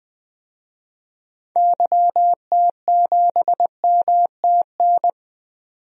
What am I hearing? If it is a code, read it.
Morse code, heard as a keyed tone, about 20 wpm, 700 Hz. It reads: YT7MTN
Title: DXCC-100 Call Signs - 6 Characters - 20wpm